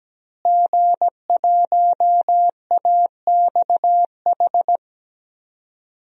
Morse code, heard as a keyed tone, about 17 wpm, 700 Hz. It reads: G1AXH
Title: DXCC-100 Call Signs - 5 Characters - Repeated 2x - 17wpm